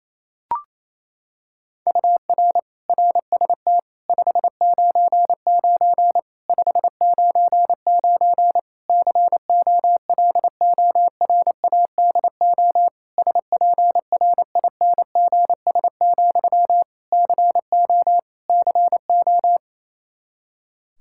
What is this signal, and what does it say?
Morse code, heard as a keyed tone, about 28 wpm, 700 Hz. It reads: UR RST 599 599 COLORADO SPRINGS, CO CO
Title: K1USN SST QSO Practice 28wpm